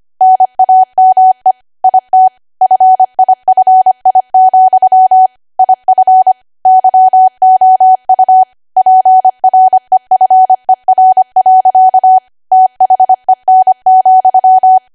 Morse code, heard as a keyed tone, about 25 wpm, 750 Hz. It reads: NAME IT FIFI, IF YOU PREFER. THEN,
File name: cwa / W1AW-Level3 / 25WPM